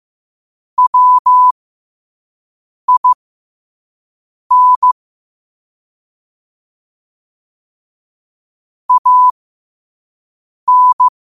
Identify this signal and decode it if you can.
Morse code, heard as a keyed tone, about 15 wpm, spaced as for 5 wpm, 1000 Hz. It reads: WIN AN
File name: qotdfiles / 5wpm